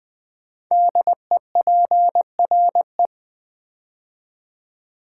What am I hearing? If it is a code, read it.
Morse code, heard as a keyed tone, about 20 wpm, 700 Hz. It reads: DEPRE
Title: Build A Word - 5 Letters At A Time - 10-Letter Words - 20wpm